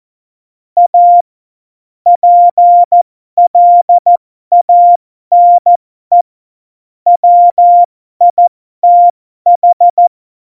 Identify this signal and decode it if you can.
Morse code, heard as a keyed tone, about 14 wpm, spaced as for 12 wpm, 700 Hz. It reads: A PLANE WITH
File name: Morsekurs / Witze